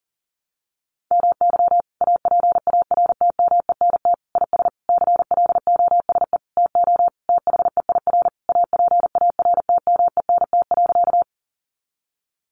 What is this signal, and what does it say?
Morse code, heard as a keyed tone, about 40 wpm, 700 Hz. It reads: MY APARTMENT IS CLOSE TO THEIR APARTMENT.
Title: Sentences from Top 900 Words 40wpm